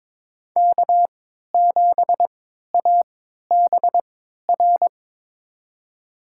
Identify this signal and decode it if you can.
Morse code, heard as a keyed tone, about 22 wpm, spaced as for 13 wpm, 700 Hz. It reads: K7ABR
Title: USA Call Signs - 5 Characters - 3x Character Spacing - 22wpm